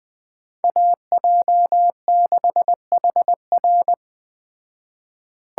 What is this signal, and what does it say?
Morse code, heard as a keyed tone, about 19 wpm, 700 Hz. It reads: AJ6HR
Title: Speed Racing - USA 2x2 Call Signs 17wpm